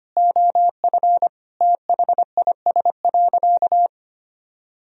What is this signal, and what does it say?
Morse code, heard as a keyed tone, about 25 wpm, 700 Hz. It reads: OF THIS.
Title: Sentences from Top 1000 Words - Review Part 1 - 25wpm